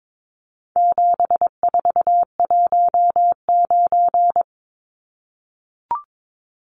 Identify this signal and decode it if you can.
Morse code, heard as a keyed tone, about 22 wpm, 700 Hz. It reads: 7419
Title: Numbers - 4-Digit 22wpm